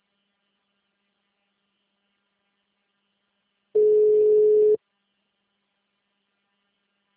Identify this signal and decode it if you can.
Ringback tone (Europe, 425 Hz).